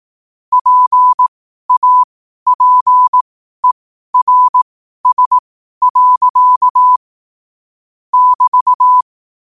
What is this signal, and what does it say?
Morse code, heard as a keyed tone, about 18 wpm, spaced as for 12 wpm, 1000 Hz. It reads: PAPERS. <BT>
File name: qotdfiles / 12wpm